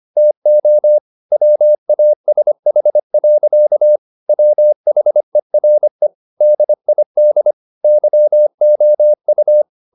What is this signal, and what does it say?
Morse code, heard as a keyed tone, about 25 wpm, 600 Hz. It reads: TO WASH. WHERE DID YOU
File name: the old man and the sea part 1 25wpm